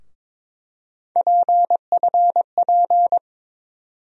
Morse code, heard as a keyed tone, about 22 wpm, 700 Hz. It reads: PFP
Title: USA Call Sign Postfixes - 3 Letters 22wpm